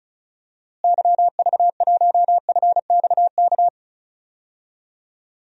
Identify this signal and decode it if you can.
Morse code, heard as a keyed tone, about 36 wpm, 700 Hz. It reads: YV1FXK
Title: DXCC-100 Call Signs - 6 Characters - Repeated 3x - 35wpm